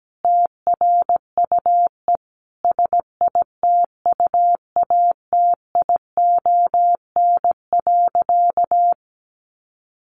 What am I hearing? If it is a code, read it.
Morse code, heard as a keyed tone, about 17 wpm, 700 Hz. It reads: TRUE SITUATION.
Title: Sentences from Top 1200 Words 17wpm